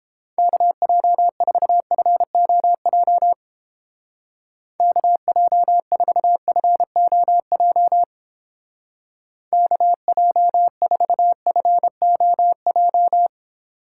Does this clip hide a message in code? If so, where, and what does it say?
Yes: Morse code from 0.4 to 13.3 s, heard as a keyed tone, about 31 wpm, 700 Hz. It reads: KJ4FOJ KJ4FOJ KJ4FOJ